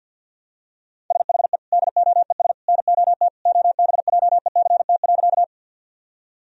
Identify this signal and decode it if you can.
Morse code, heard as a keyed tone, about 51 wpm, 700 Hz. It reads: SHE DOES NOT OBJECT.
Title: Sentences from Top 1000 Words - Review Part 3 - 50wpm